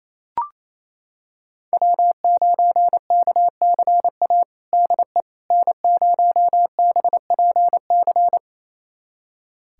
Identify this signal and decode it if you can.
Morse code, heard as a keyed tone, about 28 wpm, 700 Hz. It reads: W9KCA DE N0BPC
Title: USA Callsign DE Callsign - 5 Characters - 28wpm